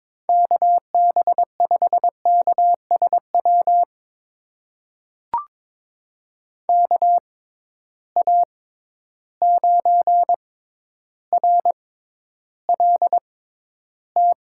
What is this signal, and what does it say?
Morse code, heard as a keyed tone, about 22 wpm, spaced as for 11 wpm, 700 Hz. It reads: KB5KSW KA9RLT
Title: USA Call Signs - 6 Characters - 6x Character Spacing - 22wpm